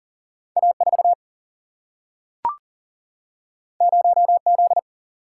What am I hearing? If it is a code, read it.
Morse code, heard as a keyed tone, about 40 wpm, 700 Hz. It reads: A4 0Z
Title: Single-Letter Single-Number 40wpm